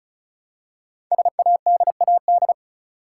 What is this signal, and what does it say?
Morse code, heard as a keyed tone, about 34 wpm, 700 Hz. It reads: SADAD
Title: Code Groups - 5 Letters - 35wpm